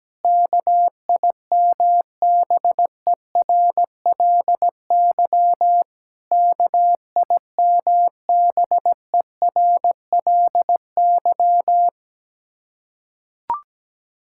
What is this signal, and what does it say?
Morse code, heard as a keyed tone, about 17 wpm, 700 Hz. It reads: KIMBERLY KIMBERLY
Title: Practice QSO - Exchange 1 - Beginning - 17wpm